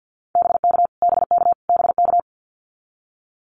Morse code, heard as a keyed tone, about 50 wpm, 700 Hz. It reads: BK BK BK
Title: Abbreviations Repeated 3x - 50wpm